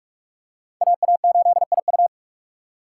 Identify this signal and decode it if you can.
Morse code, heard as a keyed tone, about 46 wpm, 690 Hz. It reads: AA8IU
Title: USA Call Signs - 5 Characters - 2x Character Spacing - 45wpm